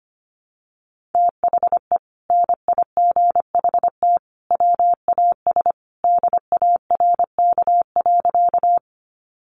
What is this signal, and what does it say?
Morse code, heard as a keyed tone, about 25 wpm, 700 Hz. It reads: THE NIGHT WAS DARK.
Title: Sentences from Top 500 Words 25wpm